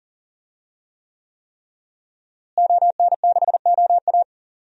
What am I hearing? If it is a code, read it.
Morse code, heard as a keyed tone, about 40 wpm, 700 Hz. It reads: ON6OA
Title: DXCC-100 Call Signs - 5 Characters - Repeated 3x - 40wpm